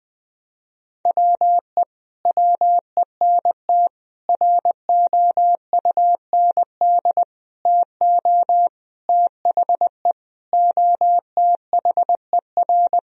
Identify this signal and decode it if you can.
Morse code, heard as a keyed tone, about 20 wpm, 700 Hz. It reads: WE WENT ROUND TO THE OTHER
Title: Sentences from Top 1000 Words - Review Part 1 - 20wpm